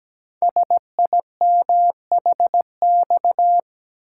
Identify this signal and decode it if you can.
Morse code, heard as a keyed tone, about 17 wpm, 700 Hz. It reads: SIMHX
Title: Code Groups - 5 Letters - 17wpm